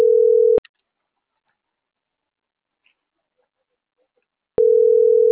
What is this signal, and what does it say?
Ringback tone (North America), 440 + 480 Hz.
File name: CENTRAL MAERSK / out luis.herrera